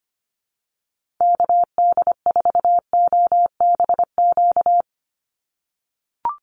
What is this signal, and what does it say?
Morse code, heard as a keyed tone, about 25 wpm, 700 Hz. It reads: KD4OBQ
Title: USA Call Signs 25wpm